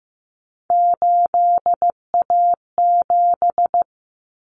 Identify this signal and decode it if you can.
Morse code, heard as a keyed tone, about 15 wpm, 700 Hz. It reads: 8A7
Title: Common Cut Numbers - 3-Digit 15wpm